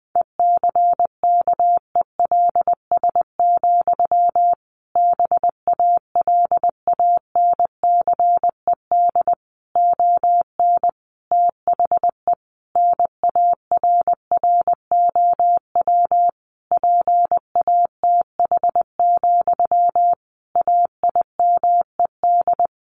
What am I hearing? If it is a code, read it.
Morse code, heard as a keyed tone, about 20 wpm, 700 Hz. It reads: ECKELS, BALANCED ON THE NARROW PATH, AIMED
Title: A Sound Of Thunder 20wpm